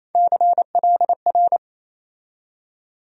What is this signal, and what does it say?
Morse code, heard as a keyed tone, about 28 wpm, 700 Hz. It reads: CLR
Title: USA Call Sign Postfixes - 3 Letters 28wpm